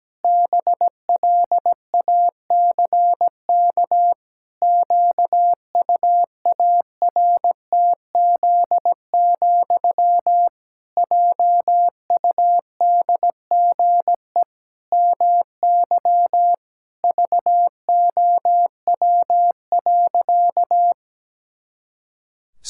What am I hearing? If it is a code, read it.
Morse code, heard as a keyed tone, about 17 wpm, 700 Hz. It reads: BLACK QUARTZ, JUDGE MY VOW.
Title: Pangram Challenge v2 - 17wpm